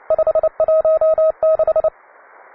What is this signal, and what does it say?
Morse code, heard as a keyed tone, about 29 wpm, 630 Hz. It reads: 516